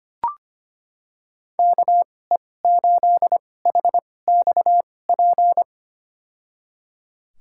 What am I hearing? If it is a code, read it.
Morse code, heard as a keyed tone, about 25 wpm, spaced as for 18 wpm, 700 Hz. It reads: KE8HXP
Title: USA Call Signs - 6 Characters - 2x Character Spacing - 25wpm